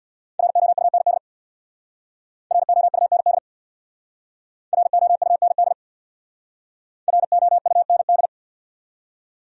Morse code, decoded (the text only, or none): ROUND ROUND ROUND ROUND